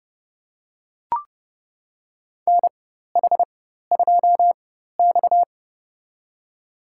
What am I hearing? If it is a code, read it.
Morse code, heard as a keyed tone, about 32 wpm, spaced as for 14 wpm, 700 Hz. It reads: NH2X
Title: USA Call Signs - 4 Characters - 4x Character Spacing - 30wpm